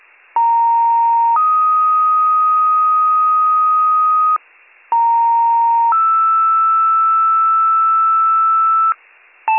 Two-tone page, heard as a one-second tone, then a three-second tone, about 930 then 1250 Hz, about 930 then 1320 Hz. It is heard from 0.4 to 8.9 s.